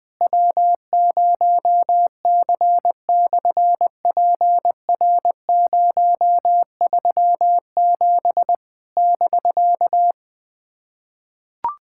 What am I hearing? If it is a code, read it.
Morse code, heard as a keyed tone, about 20 wpm, 700 Hz. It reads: W0C/PR037 <BK>
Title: SOTA QSO Practice 20wpm